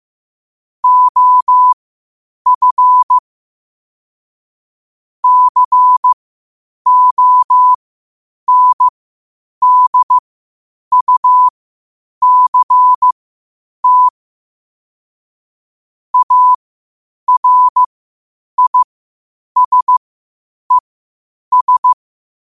Morse code, heard as a keyed tone, about 15 wpm, spaced as for 8 wpm, 1000 Hz. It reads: OF CONDUCT ARISES
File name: qotdfiles / 8wpm